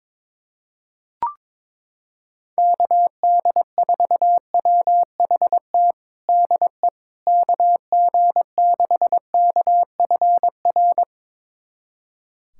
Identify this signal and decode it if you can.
Morse code, heard as a keyed tone, about 22 wpm, 700 Hz. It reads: KD4WHT DE KG6KFR
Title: USA Callsign DE Callsign - 6 Characters - 22wpm